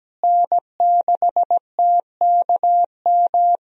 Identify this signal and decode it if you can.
Morse code, heard as a keyed tone, about 17 wpm, 700 Hz. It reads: N6TKM
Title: USA Call Signs - 5 Characters - Repeated 2x - 17wpm